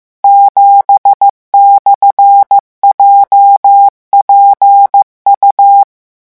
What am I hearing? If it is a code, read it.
Morse code, heard as a keyed tone, about 15 wpm, 800 Hz. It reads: 7/JPU
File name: morse mp3 / revision13-vitesse 14 mots